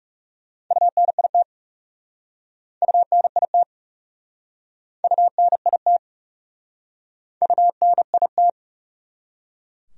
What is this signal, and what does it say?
Morse code, heard as a keyed tone, about 40 wpm, 700 Hz. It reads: UNIT UNIT UNIT UNIT